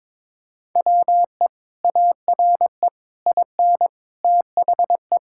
Morse code, heard as a keyed tone, about 21 wpm, 700 Hz. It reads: WE ARE IN THE
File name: practice / 22wpm / Sets of 7 Words - 22wpm